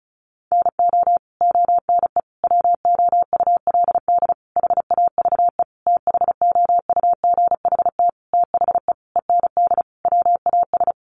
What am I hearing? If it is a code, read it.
Morse code, heard as a keyed tone, about 34 wpm, 700 Hz. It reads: NO ONE WOULD HAVE THOUGHT THE END WAS
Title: Sentences from Top 300 Words 35wpm